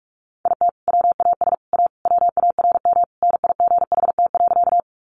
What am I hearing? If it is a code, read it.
Morse code, heard as a keyed tone, about 44 wpm, 700 Hz. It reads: IT WAS A WARM NIGHT.
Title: Sentences from Top 500 Words 45wpm